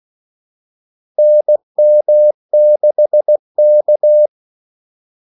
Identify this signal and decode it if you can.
Morse code, heard as a keyed tone, about 16 wpm, 600 Hz. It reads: NM6K
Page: Kaldesignaler 16 wpm